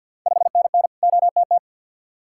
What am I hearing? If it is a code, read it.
Morse code, heard as a keyed tone, about 51 wpm, 690 Hz. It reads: 5NN OTT
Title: Ontario QSO Party Exchange 50wpm